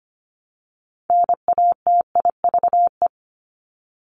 Morse code, heard as a keyed tone, about 25 wpm, 700 Hz. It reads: NATIVE
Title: Top 1751-2000 Words 25wpm